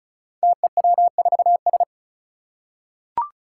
Morse code, heard as a keyed tone, about 36 wpm, 700 Hz. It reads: TEW4S
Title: Five Challenging Letter-Number Combinations 35wpm